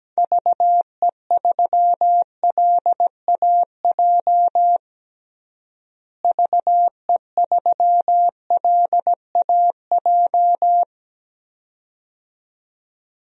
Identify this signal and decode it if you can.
Morse code, heard as a keyed tone, about 17 wpm, 700 Hz. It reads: VE3LAJ VE3LAJ
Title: DXCC-100 Call Signs - 6 Characters - Repeated 2x - 17wpm